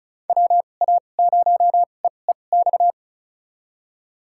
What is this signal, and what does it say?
Morse code, heard as a keyed tone, about 36 wpm, spaced as for 25 wpm, 700 Hz. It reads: WA0EEX
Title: USA Call Signs - 6 Characters - 2x Character Spacing - 35wpm